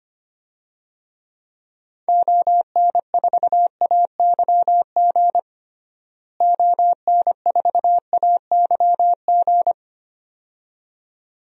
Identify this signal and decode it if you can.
Morse code, heard as a keyed tone, about 25 wpm, 700 Hz. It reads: ON4AYG ON4AYG
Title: DXCC-100 Call Signs - 6 Characters - Repeated 2x - 25wpm